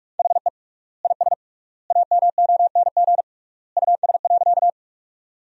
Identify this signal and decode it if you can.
Morse code, heard as a keyed tone, about 46 wpm, 700 Hz. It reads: HE IS AMONG US.
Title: 3x Word-Spacing - Sentences from Top 500 Words - 45wpm